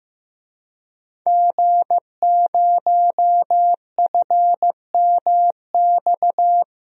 Morse code, heard as a keyed tone, about 15 wpm, 700 Hz. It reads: G0FMX
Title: DXCC-100 Call Signs - 5 Characters - Repeated 2x - 15wpm